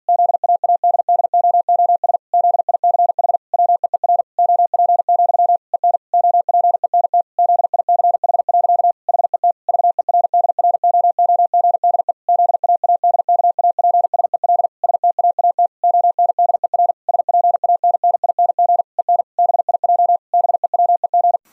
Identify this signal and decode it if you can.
Morse code, heard as a keyed tone, about 48 wpm, 700 Hz. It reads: ZAADDOOS ZICH WEER OP, EN OPENT ZICH. HET VERDROOGDE ZAADKAPSEL STAAT ONDER SPANNING EN BIJ BEWEG
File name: Zendamateurnieuws en een stukje geseinde morse